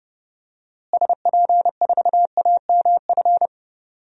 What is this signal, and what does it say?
Morse code, heard as a keyed tone, about 30 wpm, 700 Hz. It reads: SP4AMF
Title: DXCC-100 Call Signs - 6 Characters - Repeated 2x - 30wpm